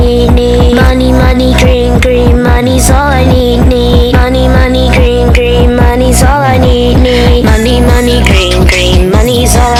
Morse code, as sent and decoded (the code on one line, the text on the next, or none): none